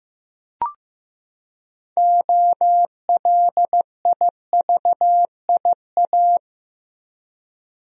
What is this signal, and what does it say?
Morse code, heard as a keyed tone, about 15 wpm, 700 Hz. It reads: OLIVIA
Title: USA Common Names - 15wpm